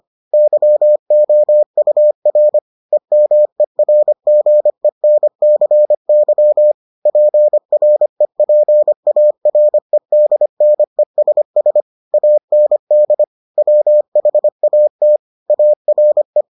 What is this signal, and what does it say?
Morse code, heard as a keyed tone, about 25 wpm, 600 Hz. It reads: YOUR EMERGENCY PREPAREDNESS AND WHAT ARE